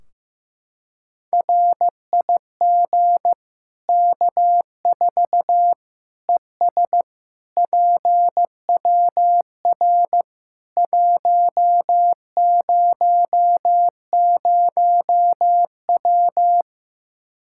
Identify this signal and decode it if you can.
Morse code, heard as a keyed tone, about 15 wpm, 700 Hz. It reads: RIG K4 ES PWR 100W
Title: Practice QSO - Exchange 2 - Info - 15wpm